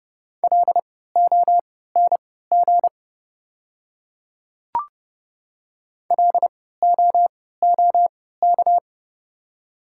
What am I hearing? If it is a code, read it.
Morse code, heard as a keyed tone, about 30 wpm, spaced as for 17 wpm, 700 Hz. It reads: LONG LOOK
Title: 3x Character Spacing - Top 200 Words - 30wpm